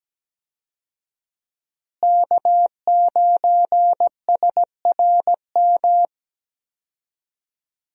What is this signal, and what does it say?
Morse code, heard as a keyed tone, about 17 wpm, 700 Hz. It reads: K9SRM
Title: USA Call Signs - 5 Characters - Repeated 2x - 17wpm